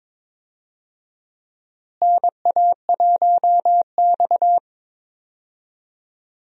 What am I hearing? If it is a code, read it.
Morse code, heard as a keyed tone, about 21 wpm, 700 Hz. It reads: NA1X
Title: USA Call Signs - 4 Characters - Repeated 2x - 22wpm